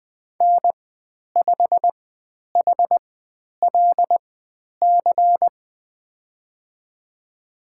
Morse code, heard as a keyed tone, about 20 wpm, spaced as for 10 wpm, 700 Hz. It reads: N5HLC
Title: 102 Lesson 17g - Letter u - Call Signs 20wpm